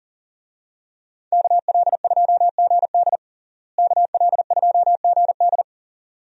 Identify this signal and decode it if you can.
Morse code, heard as a keyed tone, about 40 wpm, 700 Hz. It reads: KL2GD KL2GD
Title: USA Call Signs - 5 Characters - Repeated 2x - 40wpm